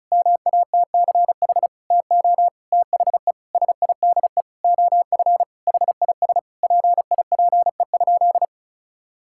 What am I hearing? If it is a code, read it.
Morse code, heard as a keyed tone, about 34 wpm, 700 Hz. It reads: MATCH TO THE SIDE OF HIS PIPE?
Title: Sentences from Top 1600 Words 35wpm